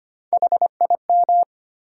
Morse code, heard as a keyed tone, about 25 wpm, 700 Hz. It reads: HIM
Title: Three Letter Words - 25wpm